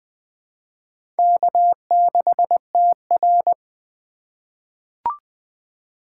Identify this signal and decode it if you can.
Morse code, heard as a keyed tone, about 20 wpm, 700 Hz. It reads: K6TR
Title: USA Call Signs - 4 Characters - 3x Character Spacing - 20wpm